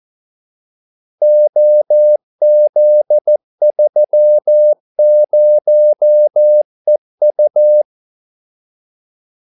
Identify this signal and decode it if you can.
Morse code, heard as a keyed tone, about 14 wpm, 600 Hz. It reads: OZ30EU
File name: Callsigns-14wpm